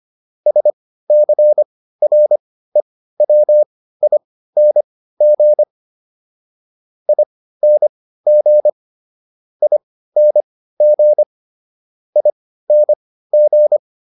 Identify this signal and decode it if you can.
Morse code, heard as a keyed tone, about 25 wpm, spaced as for 15 wpm, 600 Hz. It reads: SCREWING ING ING ING